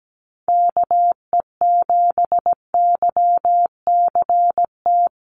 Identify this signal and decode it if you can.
Morse code, heard as a keyed tone, about 17 wpm, 700 Hz. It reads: KE7YCT
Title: USA Call Signs Repeated 2x 17wpm